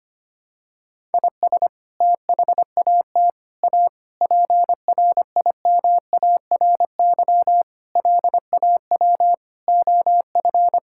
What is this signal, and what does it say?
Morse code, heard as a keyed tone, about 25 wpm, 700 Hz. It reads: IS THAT A PRIMARY LAW OF
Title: Sentences from Top 1000 Words - Review Part 5 - 25wpm